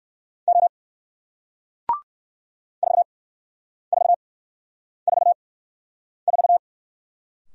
Morse code, heard as a keyed tone, about 57 wpm, spaced as for 9 wpm, 700 Hz. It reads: O 4444